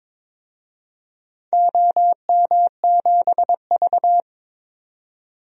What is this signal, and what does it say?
Morse code, heard as a keyed tone, about 22 wpm, 700 Hz. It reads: OM7V
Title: DXCC-100 Call Signs - 4 Characters - 22wpm